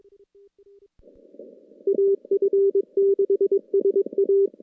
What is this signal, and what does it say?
Morse code, heard as a keyed tone, about 22 wpm, 390 Hz. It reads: AF6SA
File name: contests / CW-SS-k5tr- -2300